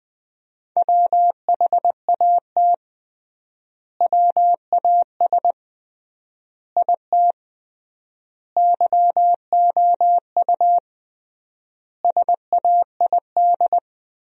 Morse code, heard as a keyed tone, about 21 wpm, 700 Hz. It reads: WHAT WAS IT YOU SAID
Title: 3x Word-Spacing - Sentences from Top 200 Words - 20wpm